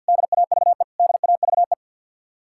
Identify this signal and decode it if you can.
Morse code, heard as a keyed tone, about 66 wpm, spaced as for 58 wpm, 700 Hz. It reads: DAVE DAVE
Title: Speed-Racing Common QSO Phrases - 50wpm